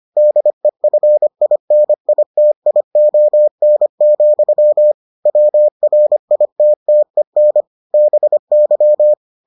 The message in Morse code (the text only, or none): DEFINITION, WRITTEN BY